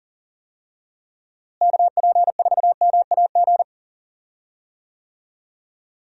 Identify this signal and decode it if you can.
Morse code, heard as a keyed tone, about 40 wpm, 700 Hz. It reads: KP4MAG
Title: USA Call Signs - 6 Characters - Repeated 3x - 40wpm